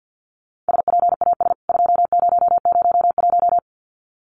Morse code, heard as a keyed tone, about 50 wpm, 700 Hz. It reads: SPAS 2001
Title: CWT Exchange 50wpm